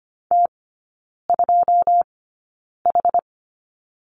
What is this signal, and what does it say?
Morse code, heard as a keyed tone, about 25 wpm, spaced as for 9 wpm, 700 Hz. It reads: T2H
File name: course / 25:10wpm / 093 Lesson 16e - Letter c - Three Characters 25wpm